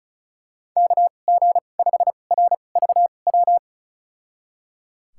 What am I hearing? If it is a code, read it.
Morse code, heard as a keyed tone, about 34 wpm, spaced as for 25 wpm, 700 Hz. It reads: KG5RVW
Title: USA Call Signs - 6 Characters - 2x Character Spacing - 35wpm